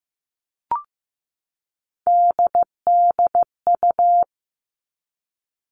Morse code, heard as a keyed tone, about 15 wpm, 700 Hz. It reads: DDU
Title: All Cut Numbers - 3-Digit 15wpm